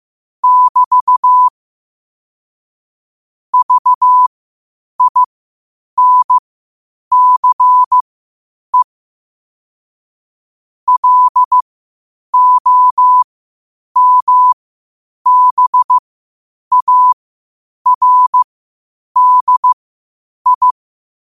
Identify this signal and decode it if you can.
Morse code, heard as a keyed tone, about 15 wpm, spaced as for 8 wpm, 1000 Hz. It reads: <BT> VINCE LOMBARDI
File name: qotdfiles / 8wpm